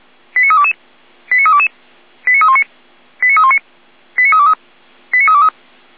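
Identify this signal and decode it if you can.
Five-tone selective call (ZVEI), heard as quick runs of five tones, tones of about 70 ms. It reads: 89322 89322 89319 89319 89323 89323
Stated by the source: ZVEI five-tone call 89015 received on an SDR.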